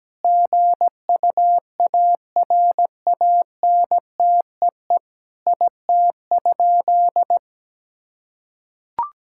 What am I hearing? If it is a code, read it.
Morse code, heard as a keyed tone, about 17 wpm, 700 Hz. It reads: GUARANTEE IT?
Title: Sentences from Top 1600 Words 17wpm